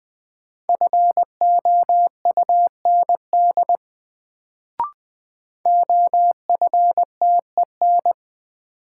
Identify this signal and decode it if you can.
Morse code, heard as a keyed tone, about 20 wpm, 700 Hz. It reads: FOUND OFTEN
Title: Sentences from Top 300 Words - New Word Drill - 20wpm